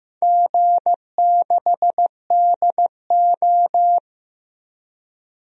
Morse code, heard as a keyed tone, about 15 wpm, 700 Hz. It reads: G6DO
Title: DXCC-100 Call Signs - 4 Characters - Repeated 2x - 15wpm